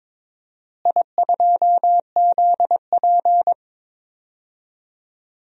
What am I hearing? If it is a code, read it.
Morse code, heard as a keyed tone, about 22 wpm, 700 Hz. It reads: I2ZP
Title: DXCC-100 Call Signs - 4 Characters - Repeated 3x - 22wpm